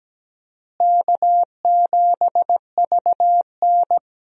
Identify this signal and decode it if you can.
Morse code, heard as a keyed tone, about 17 wpm, 700 Hz. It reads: K7VN
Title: USA Call Signs - 4 Characters - Repeated 3x - 17wpm